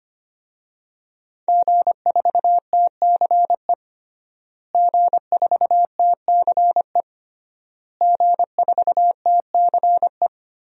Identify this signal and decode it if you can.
Morse code, heard as a keyed tone, about 25 wpm, 700 Hz. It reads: G4TCE G4TCE G4TCE
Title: DXCC-100 Call Signs - 5 Characters - Repeated 3x - 25wpm